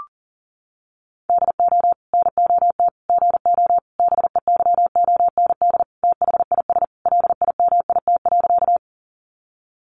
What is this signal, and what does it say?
Morse code, heard as a keyed tone, about 40 wpm, 700 Hz. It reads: DO NOT GO BEYOND THIS LIMIT.
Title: Sentences from Top 1500 Words 40wpm